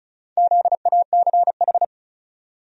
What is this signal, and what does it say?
Morse code, heard as a keyed tone, about 34 wpm, 700 Hz. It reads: ZACH
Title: USA Common Names - Max 5 Letters - 35wpm